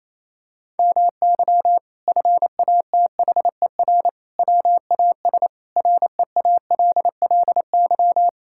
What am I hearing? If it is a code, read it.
Morse code, heard as a keyed tone, about 28 wpm, 700 Hz. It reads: MY FATHER WAS REALLY
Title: Sentences from Top 1000 Words - Review Part 6 - 28wpm